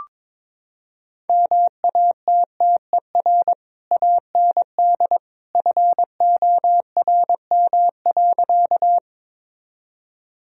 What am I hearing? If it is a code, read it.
Morse code, heard as a keyed tone, about 22 wpm, 700 Hz. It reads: MATTER AND FORM.
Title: Sentences from Top 1000 Words - Review Part 1 - 22wpm